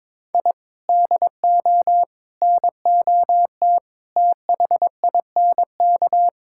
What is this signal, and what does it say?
Morse code, heard as a keyed tone, about 22 wpm, 700 Hz. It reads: I DO NOT THINK
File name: practice / 22wpm / Sentences with 3-Word Phrases - 22wpm